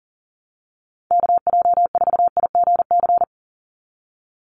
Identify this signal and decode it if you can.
Morse code, heard as a keyed tone, about 41 wpm, 700 Hz. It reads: KJ4IGC
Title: USA Call Signs 40wpm